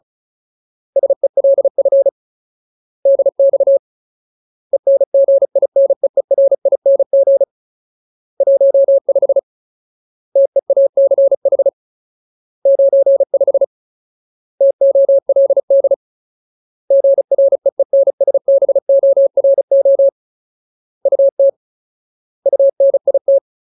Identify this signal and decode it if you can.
Morse code, heard as a keyed tone, about 37 wpm, 550 Hz. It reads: SELF DX ENGINEERING 15 TEACH 95 TOLD GREENSBORO UT UNIT
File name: Ultimate Word List / 35WPM Ultimate